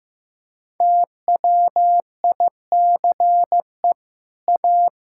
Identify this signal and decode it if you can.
Morse code, heard as a keyed tone, about 15 wpm, 700 Hz. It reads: TWICE A
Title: Sentences from Top 1600 Words 15wpm